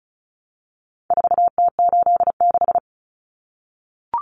Morse code, heard as a keyed tone, about 34 wpm, 700 Hz. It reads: 4T86
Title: Common Cut Numbers - 4-Digit 35wpm